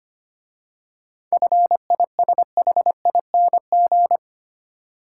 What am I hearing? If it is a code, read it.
Morse code, heard as a keyed tone, about 28 wpm, 700 Hz. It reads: FISHING
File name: Easy Words - Speed-Racing with Hints - 25wpm